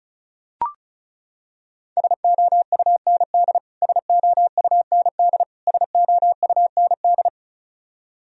Morse code, heard as a keyed tone, about 34 wpm, 700 Hz. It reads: SOUND SOUND SOUND
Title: Top 100 Words Repeated 35wpm